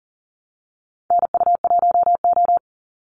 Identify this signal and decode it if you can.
Morse code, heard as a keyed tone, about 40 wpm, 700 Hz. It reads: NU1O
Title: Call Signs - 40wpm